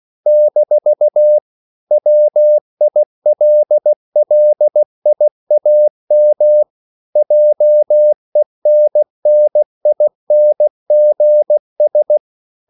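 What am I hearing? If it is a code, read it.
Morse code, heard as a keyed tone, about 16 wpm, 600 Hz. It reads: - WILLIAM JENNINGS
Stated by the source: English quotations sent in CW as practice at 16 words per minute.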